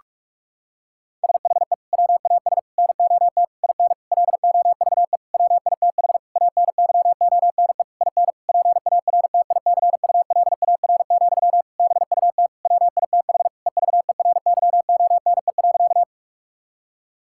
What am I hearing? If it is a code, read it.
Morse code, heard as a keyed tone, about 46 wpm, 700 Hz. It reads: SHE WAS NOT IN LOVE WITH ANYONE IN PARTICULAR, BUT WITH EVERYONE.
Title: Sentences from Top 1000 Words - Review Part 5 - 45wpm